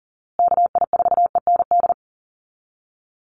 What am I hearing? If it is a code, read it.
Morse code, heard as a keyed tone, about 40 wpm, 700 Hz. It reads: KI4END
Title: USA Call Signs Repeated 3x 40wpm